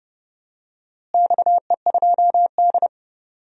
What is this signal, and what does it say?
Morse code, heard as a keyed tone, about 30 wpm, 700 Hz. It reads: XE2D
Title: DXCC-100 Call Signs - 4 Characters - Repeated 3x - 30wpm